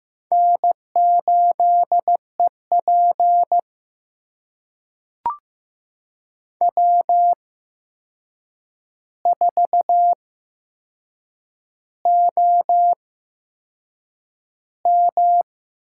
Morse code, heard as a keyed tone, about 15 wpm, spaced as for 6 wpm, 700 Hz. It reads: N8EP W4OM
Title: USA Call Signs - 8x Character Spacing - 15wpm